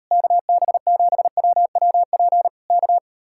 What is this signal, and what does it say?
Morse code, heard as a keyed tone, about 38 wpm, 700 Hz. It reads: KB7WWP K